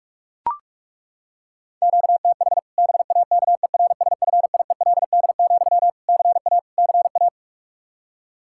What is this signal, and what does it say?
Morse code, heard as a keyed tone, about 43 wpm, 700 Hz. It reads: QTH BAKERSFIELD, CA CA
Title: Practice QSO - Exchange 1 - Beginning - 45wpm